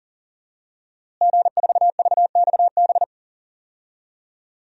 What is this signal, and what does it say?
Morse code, heard as a keyed tone, about 37 wpm, 700 Hz. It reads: G4VXB
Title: DXCC-100 Call Signs - 5 Characters - Repeated 2x - 40wpm